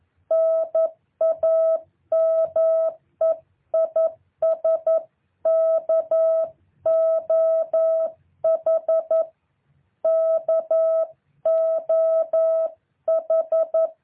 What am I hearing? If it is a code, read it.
Morse code, heard as a keyed tone, about 11 wpm, 650 Hz. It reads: NAME IS KOH KOH
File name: jh8cbh / 2023cw